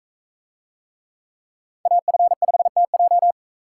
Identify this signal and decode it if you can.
Morse code, heard as a keyed tone, about 40 wpm, 700 Hz. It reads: AF5TJ